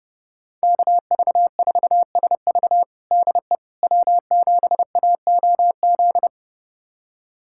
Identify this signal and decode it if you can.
Morse code, heard as a keyed tone, about 30 wpm, 700 Hz. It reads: KV4SV DE W7AOZ
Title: USA Callsign DE Callsign - 5 Characters - 30wpm